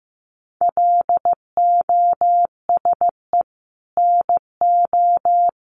Morse code, heard as a keyed tone, about 15 wpm, 700 Hz. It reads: LOSE NO
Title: Sentences from Top 1100 Words 15wpm